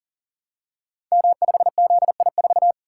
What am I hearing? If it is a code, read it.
Morse code, heard as a keyed tone, about 38 wpm, 690 Hz. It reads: M5ZI4
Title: Code Groups - 5 Letter:Numbers - 40wpm